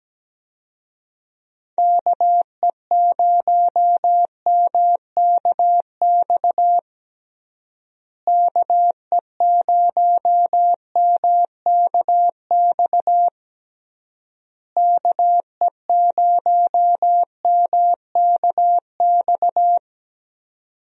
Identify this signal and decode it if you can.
Morse code, heard as a keyed tone, about 17 wpm, 700 Hz. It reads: KE0MKX KE0MKX KE0MKX
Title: USA Call Signs - 6 Characters - Repeated 3x - 17wpm